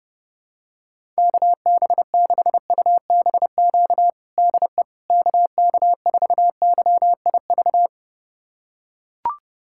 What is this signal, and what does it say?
Morse code, heard as a keyed tone, about 30 wpm, 700 Hz. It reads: KB6UBQ DE KK4YIV
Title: USA Callsign DE Callsign - 6 Characters - 30wpm